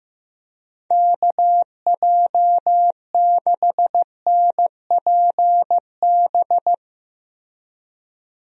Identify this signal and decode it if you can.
Morse code, heard as a keyed tone, about 15 wpm, 700 Hz. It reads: KJ6NPB
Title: USA Call Signs - 6 Characters - Repeated 3x - 15wpm